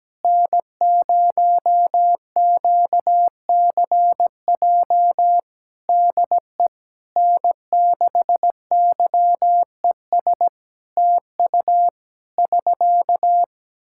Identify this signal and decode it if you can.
Morse code, heard as a keyed tone, about 17 wpm, 700 Hz. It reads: N0QCJ DE N6YES TU <SK>
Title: K1USN SST QSO Practice 17wpm